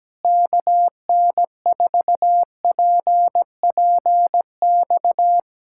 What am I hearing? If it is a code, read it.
Morse code, heard as a keyed tone, about 17 wpm, 700 Hz. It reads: KN4PPX
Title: Speed-Racing Common QSO Phrases Encore - 17wpm